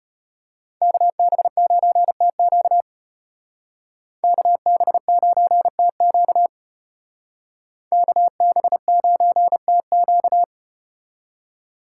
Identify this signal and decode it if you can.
Morse code, heard as a keyed tone, about 33 wpm, 700 Hz. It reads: KB9TQ KB9TQ KB9TQ